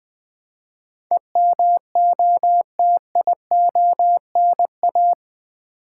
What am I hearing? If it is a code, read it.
Morse code, heard as a keyed tone, about 20 wpm, 700 Hz. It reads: EMOTIONA
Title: Build A Word - 4 Letters At A Time - 11-Letter Words - 20wpm